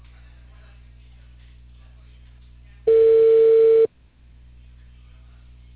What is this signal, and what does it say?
Telephone call progress tone (China, 450 Hz), heard as a telephone network's tone, about 450 Hz. Ringback tone.